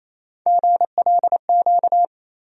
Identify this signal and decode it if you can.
Morse code, heard as a keyed tone, about 28 wpm, 700 Hz. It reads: GLQ
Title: USA Call Sign Postfixes - 3 Letters 28wpm